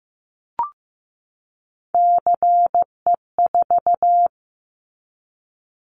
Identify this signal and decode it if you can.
Morse code, heard as a keyed tone, about 15 wpm, 700 Hz. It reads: CE4
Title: Callsign Prefixes for Top 100 DXCC - 15wpm